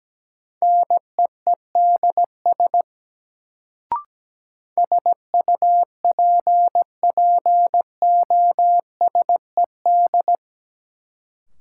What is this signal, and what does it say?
Morse code, heard as a keyed tone, about 17 wpm, 700 Hz. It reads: NEEDS SUPPOSED
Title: Sets of 3 Words - New Word Drill - 17wpm